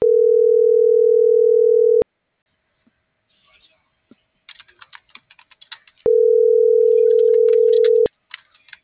Ringback tone (North America), 440 + 480 Hz.